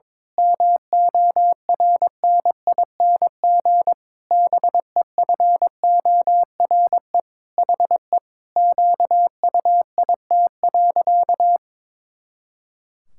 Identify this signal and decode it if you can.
Morse code, heard as a keyed tone, about 22 wpm, 700 Hz. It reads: MORNING BEFORE HE QUIT.
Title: Pangram Challenge v2 - 22wpm